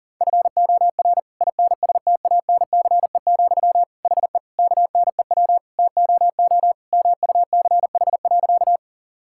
Morse code, heard as a keyed tone, about 40 wpm, 700 Hz. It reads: FOR INSTANCE, HE KNEW TOO MUCH.
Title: Sentences from Top 1000 Words - Review Part 5 - 40wpm